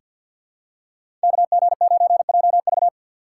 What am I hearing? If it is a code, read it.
Morse code, heard as a keyed tone, about 47 wpm, 700 Hz. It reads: KG9JV
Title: USA Call Signs - 5 Characters - 4x Character Spacing - 50wpm